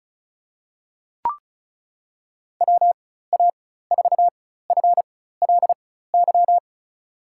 Morse code, heard as a keyed tone, about 36 wpm, spaced as for 16 wpm, 700 Hz. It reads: WA4FLY
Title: USA Call Signs - 6 Characters - 4x Character Spacing - 35wpm